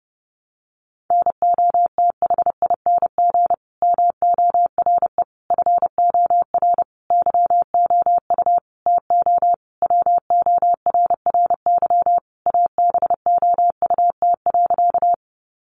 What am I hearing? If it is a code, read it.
Morse code, heard as a keyed tone, about 30 wpm, 700 Hz. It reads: NOTHING MORE FOR YOU TO WORRY ABOUT.
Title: Sentences from Top 1100 Words 30wpm